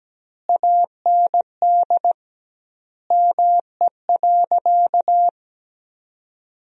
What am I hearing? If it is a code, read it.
Morse code, heard as a keyed tone, about 17 wpm, 700 Hz. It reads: AND ME.
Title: 2x Word-Spacing - Sentences from Top 500 Words - 17wpm